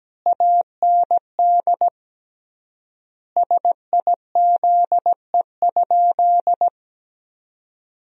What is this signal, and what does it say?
Morse code, heard as a keyed tone, about 17 wpm, 700 Hz. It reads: AND SIZE?
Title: 3x Word-Spacing - Sentences from Top 500 Words - 17wpm